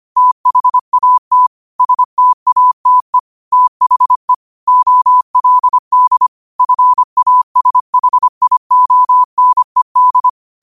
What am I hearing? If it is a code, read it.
Morse code, heard as a keyed tone, about 25 wpm, 1000 Hz. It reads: THAT STATE THE OLD FASHIONED
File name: qotdfiles / 25wpm